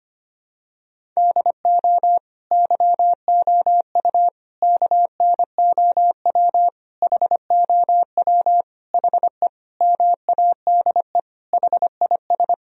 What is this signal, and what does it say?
Morse code, heard as a keyed tone, about 25 wpm, 700 Hz. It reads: DO YOU KNOW HOW HE MADE HIS
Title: Sentences from Top 1000 Words - Review Part 1 - 25wpm